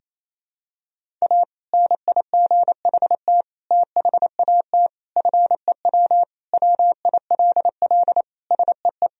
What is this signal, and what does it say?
Morse code, heard as a keyed tone, about 28 wpm, 700 Hz. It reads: A NIGHT THAT FEW WILL SEE
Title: Sentences from Top 1000 Words - Review Part 1 - 28wpm